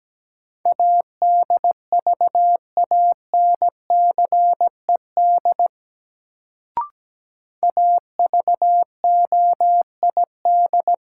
Morse code, heard as a keyed tone, about 17 wpm, 700 Hz. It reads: ADVANCED AVOID
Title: Sentences from Top 1200 Words - New Word Drill Alt - 17wpm